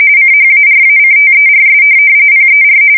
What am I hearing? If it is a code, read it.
RTTY, about 45.45 baud. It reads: CHARACTERS PER
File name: fsk170 / teletype-corp